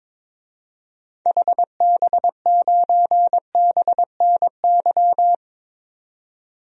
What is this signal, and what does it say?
Morse code, heard as a keyed tone, about 21 wpm, 700 Hz. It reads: HB9BNY
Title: DXCC-100 Call Signs - 6 Characters - 22wpm